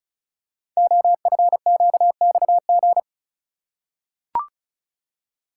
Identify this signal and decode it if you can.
Morse code, heard as a keyed tone, about 35 wpm, 700 Hz. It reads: OFQXG
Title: Code Groups - 5 Letters - 35wpm